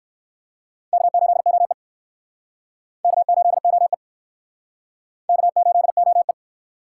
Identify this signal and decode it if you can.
Morse code, heard as a keyed tone, about 59 wpm, 700 Hz. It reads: K8OE K8OE K8OE